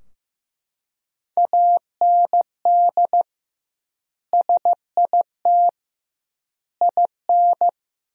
Morse code, heard as a keyed tone, about 16 wpm, 700 Hz. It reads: AND SIT IN
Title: 2x Word-Spacing - Sentences from Top 500 Words - 15wpm